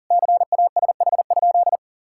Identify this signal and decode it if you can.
Morse code, heard as a keyed tone, about 37 wpm, 700 Hz. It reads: CASH?
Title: Sentences from Top 1000 Words - Review Part 5 - 40wpm